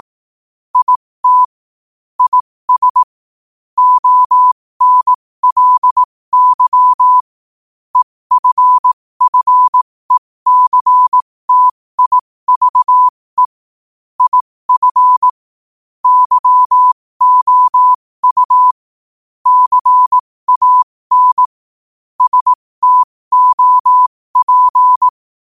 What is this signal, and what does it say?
Morse code, heard as a keyed tone, about 18 wpm, spaced as for 15 wpm, 1000 Hz. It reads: IT IS ONLY EFFECTIVE IF YOU CAN STOP
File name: qotdfiles / 15wpm